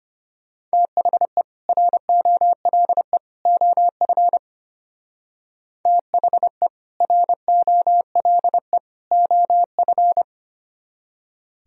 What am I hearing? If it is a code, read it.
Morse code, heard as a keyed tone, about 32 wpm, 700 Hz. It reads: THE ROLE OF THE ROLE OF